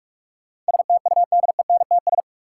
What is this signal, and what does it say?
Morse code, heard as a keyed tone, about 46 wpm, 700 Hz. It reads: STUDENTS
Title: Sets of 5 Words Encore - New Word Drill - 45wpm